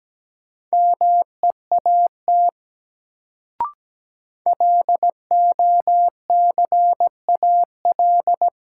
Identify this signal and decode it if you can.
Morse code, heard as a keyed tone, about 17 wpm, 700 Hz. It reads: MEAT LOCAL
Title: Sentences from Top 1000 Words - Review Part 3 - New Word Drill - 17wpm